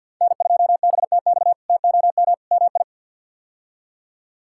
Morse code, heard as a keyed tone, about 46 wpm, 700 Hz. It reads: N2BTX TOM MI
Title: K1USN SST QSO Practice - 50wpm